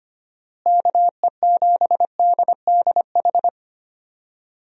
Morse code, heard as a keyed tone, about 25 wpm, 700 Hz. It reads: KE7DDH
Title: USA Call Signs - 6 Characters - Repeated 3x - 25wpm